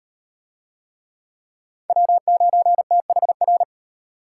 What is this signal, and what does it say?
Morse code, heard as a keyed tone, about 34 wpm, 700 Hz. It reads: W9THR